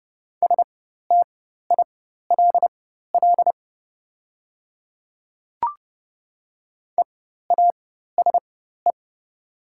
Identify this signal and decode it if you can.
Morse code, heard as a keyed tone, about 30 wpm, spaced as for 14 wpm, 700 Hz. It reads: STILL EASE
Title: 4x Character Spacing - Top 300 Words - 30wpm